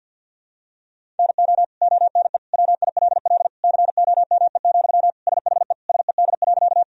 Morse code, heard as a keyed tone, about 52 wpm, 700 Hz. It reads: NO ONE WILL COME, SHE SED.
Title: Sentences from Top 100 Words and Abbreviations 50wpm